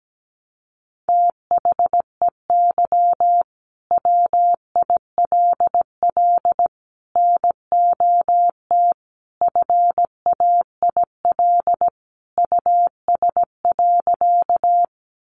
Morse code, heard as a keyed tone, about 17 wpm, 700 Hz. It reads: THEY WILL NOT FAIL US.
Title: Sentences from Top 1100 Words 17wpm